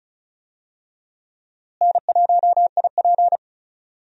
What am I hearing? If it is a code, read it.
Morse code, heard as a keyed tone, about 36 wpm, 700 Hz. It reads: N1IP
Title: USA Call Signs - 4 Characters - Repeated 3x - 35wpm